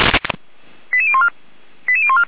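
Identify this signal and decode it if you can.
Five-tone selective call (ZVEI), heard as quick runs of five tones, tones of about 70 ms. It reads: 88014 88014
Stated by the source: Received by SDR.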